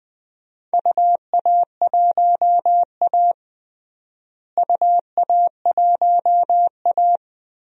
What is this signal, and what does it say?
Morse code, heard as a keyed tone, about 21 wpm, 700 Hz. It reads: UA1A UA1A
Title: DXCC-100 Call Signs - 4 Characters - Repeated 3x - 20wpm